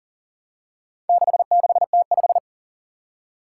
Morse code, heard as a keyed tone, about 40 wpm, 700 Hz. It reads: 66T5
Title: Four Challenging Letter-Number Combinations 40wpm